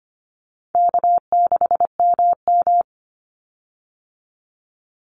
Morse code, heard as a keyed tone, about 25 wpm, 700 Hz. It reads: K6MM
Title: Call Signs Repeated 2x - 25wpm